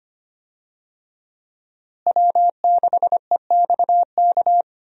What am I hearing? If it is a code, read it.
Morse code, heard as a keyed tone, about 25 wpm, 700 Hz. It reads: W6EXK
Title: USA Call Signs - 5 Characters - Repeated 2x - 25wpm